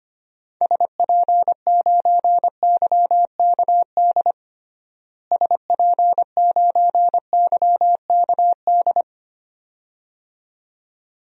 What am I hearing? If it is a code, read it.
Morse code, heard as a keyed tone, about 25 wpm, 700 Hz. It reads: SP9YKD SP9YKD
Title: DXCC-100 Call Signs - 6 Characters - Repeated 2x - 25wpm